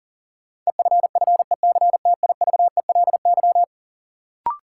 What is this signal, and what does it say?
Morse code, heard as a keyed tone, about 37 wpm, 700 Hz. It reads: EFFECTIVELY
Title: Sentences from Top 1400 Words - New Word Drill - 40wpm